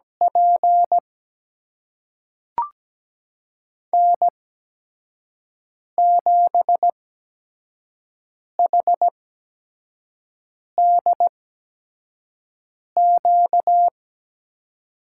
Morse code, heard as a keyed tone, about 17 wpm, spaced as for 5 wpm, 700 Hz. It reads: P N7HDQ
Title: USA Call Signs - 8x Character Spacing - 17wpm